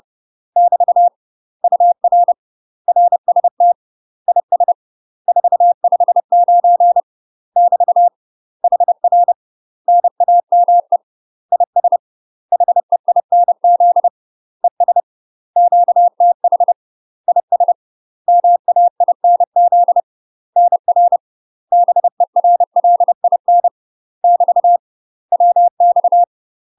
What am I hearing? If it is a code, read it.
Morse code, heard as a keyed tone, about 30 wpm, 700 Hz. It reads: <BT> UR RST IS 459 <BT> HR NAME IS HEINZ ES QTH IS MAINZ NR BERLIN <BT> WX
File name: UK MASTER QSOs TRAINING/30WPM